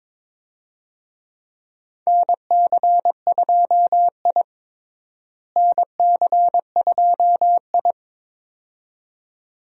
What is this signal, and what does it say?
Morse code, heard as a keyed tone, about 22 wpm, 700 Hz. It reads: NC2I NC2I
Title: USA Call Signs - 4 Characters - Repeated 2x - 22wpm